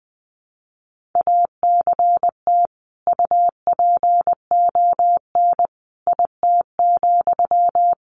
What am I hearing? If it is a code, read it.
Morse code, heard as a keyed tone, about 20 wpm, 700 Hz. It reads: ACT UPON IT,
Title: Sentences from Top 1100 Words 20wpm